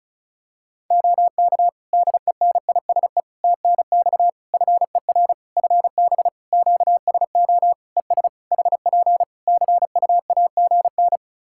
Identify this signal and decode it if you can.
Morse code, heard as a keyed tone, about 34 wpm, 700 Hz. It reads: OK DENISE TNX FER FB QSO ES HP CUAGN
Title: K1USN SST QSO Practice 35wpm